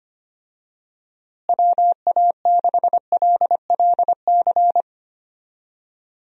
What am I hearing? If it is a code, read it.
Morse code, heard as a keyed tone, about 25 wpm, 700 Hz. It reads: WA6LLC
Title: USA Call Signs - 6 Characters - Repeated 3x - 25wpm